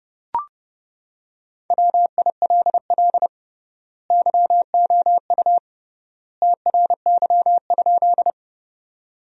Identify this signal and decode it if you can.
Morse code, heard as a keyed tone, about 30 wpm, 700 Hz. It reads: WILL YOU TRY?
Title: 3x Word-Spacing - Sentences from Top 200 Words - 30wpm